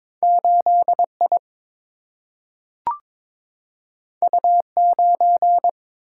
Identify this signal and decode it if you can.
Morse code, heard as a keyed tone, about 22 wpm, 700 Hz. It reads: 8I U9
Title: Single-Letter Single-Number 22wpm